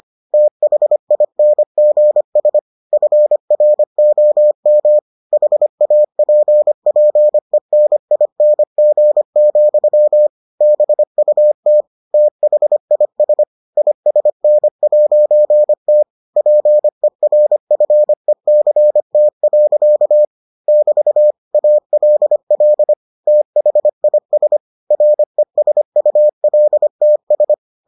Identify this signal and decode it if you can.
Morse code, heard as a keyed tone, about 25 wpm, 600 Hz. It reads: THINGS FROM HAPPENING, BUT THIS ISN'T PERFECT. <BT> ALL THIS RESULTS